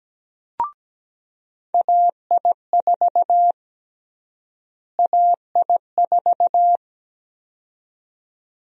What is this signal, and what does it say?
Morse code, heard as a keyed tone, about 17 wpm, 700 Hz. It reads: AI4 AI4
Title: USA Call Sign Prefixes - Repeated 2x - 17wpm